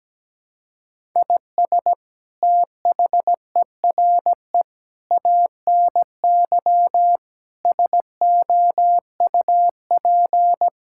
Morse code, heard as a keyed tone, about 17 wpm, 700 Hz. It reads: IS THERE ANY SOUP
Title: Sentences from Top 1000 Words - Review Part 6 - 17wpm